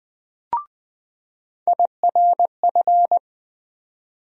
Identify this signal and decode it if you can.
Morse code, heard as a keyed tone, about 20 wpm, 700 Hz. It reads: IRF
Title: USA Call Sign Postfixes - 3 Letters 20wpm